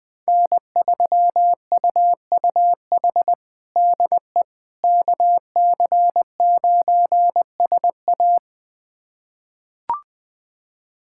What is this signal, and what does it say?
Morse code, heard as a keyed tone, about 21 wpm, 700 Hz. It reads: N3UUH DE KC9SA
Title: USA Callsign DE Callsign - 5 Characters - 20wpm